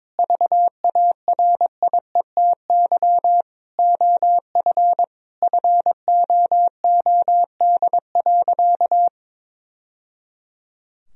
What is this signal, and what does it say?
Morse code, heard as a keyed tone, about 22 wpm, 700 Hz. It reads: VARIETY OF FOOD.
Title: Sentences from Top 1000 Words - Review Part 4 - 22wpm